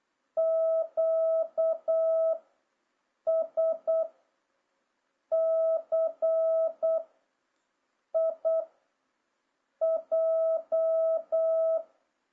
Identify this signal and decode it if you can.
Morse code, heard as a keyed tone, about 8 wpm, 650 Hz. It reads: Q S C I J
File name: jh8cbh / 2026cw